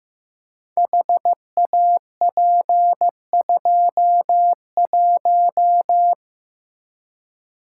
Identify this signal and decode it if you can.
Morse code, heard as a keyed tone, about 15 wpm, 700 Hz. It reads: HAP21
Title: Code Groups - 5 Letter:Numbers - 15wpm